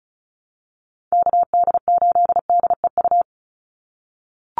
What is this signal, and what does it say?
Morse code, heard as a keyed tone, about 34 wpm, 700 Hz. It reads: KD8DEU
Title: NA SOTA Chaser Callsigns 35wpm